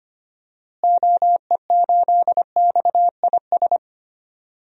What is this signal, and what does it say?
Morse code, heard as a keyed tone, about 25 wpm, 700 Hz. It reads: OE8XIS
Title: DXCC-100 Call Signs - 6 Characters - Repeated 3x - 25wpm